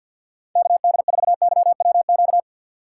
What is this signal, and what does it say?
Morse code, heard as a keyed tone, about 49 wpm, 700 Hz. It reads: KD4YWQ
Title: USA Call Signs - 6 Characters - 3x Character Spacing - 50wpm